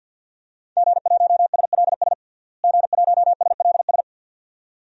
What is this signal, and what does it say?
Morse code, heard as a keyed tone, about 52 wpm, 700 Hz. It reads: G1SLS G1SLS
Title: DXCC-100 Call Signs - 5 Characters - Repeated 2x - 50wpm